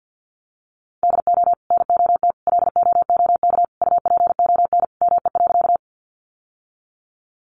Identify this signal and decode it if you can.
Morse code, heard as a keyed tone, about 50 wpm, 700 Hz. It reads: DO NOT LOOK UPON ME.
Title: Sentences from Top 1100 Words 50wpm